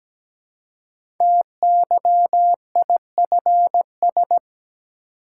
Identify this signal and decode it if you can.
Morse code, heard as a keyed tone, about 17 wpm, 700 Hz. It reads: TYIFS
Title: Code Groups - 5 Letter:Numbers - 17wpm